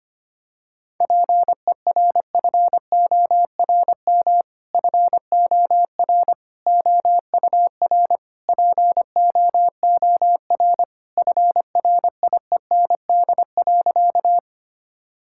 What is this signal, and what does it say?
Morse code, heard as a keyed tone, about 25 wpm, 700 Hz. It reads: PERFORM FOR OUR POOR FRIEND.
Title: Sentences from Top 1000 Words - Review Part 5 - 25wpm